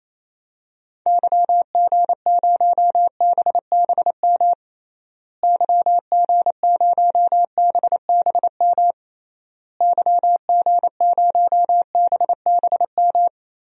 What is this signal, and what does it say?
Morse code, heard as a keyed tone, about 28 wpm, 700 Hz. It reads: YG0BBM YG0BBM YG0BBM
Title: DXCC-100 Call Signs - 6 Characters - Repeated 3x - 28wpm